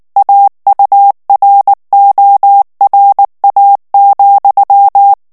Morse code, heard as a keyed tone, about 19 wpm, 800 Hz. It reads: AURORA,
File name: ANGLAIS texte clair 3 vitesse 18 mots minute